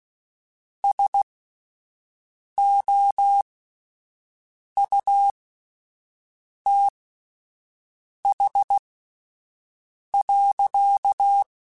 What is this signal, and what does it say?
Morse code, heard as a keyed tone, about 16 wpm, spaced as for 5 wpm, 800 Hz. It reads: SOUTH.
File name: Spaced Characters / Spaced Characters-mp3s / Weather 4 10 sp @10